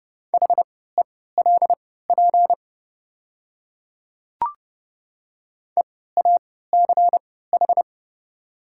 Morse code, heard as a keyed tone, about 30 wpm, spaced as for 17 wpm, 700 Hz. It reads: HELP EACH
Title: 3x Character Spacing - Top 300 Words - 30wpm